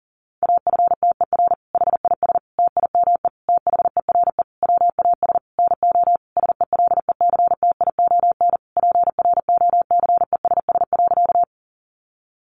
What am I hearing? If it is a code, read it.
Morse code, heard as a keyed tone, about 40 wpm, 700 Hz. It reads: AFTER HIS TIME THERE WAS NO SELECTION PROCESS.
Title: Sentences from Top 1400 Words 40wpm